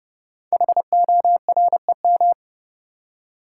Morse code, heard as a keyed tone, about 30 wpm, 700 Hz. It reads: HOREM
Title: Code Groups - 5 Letters - 30wpm